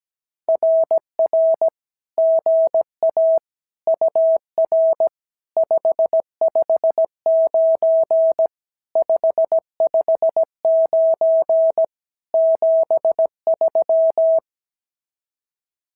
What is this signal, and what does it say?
Morse code, heard as a keyed tone, about 17 wpm, 650 Hz. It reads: RR GA UR 559 559 73
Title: SOTA QSO Practice 17wpm